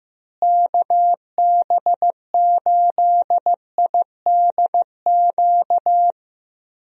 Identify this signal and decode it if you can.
Morse code, heard as a keyed tone, about 15 wpm, 700 Hz. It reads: KB8IDQ
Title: USA Call Signs - 6 Characters - Repeated 3x - 15wpm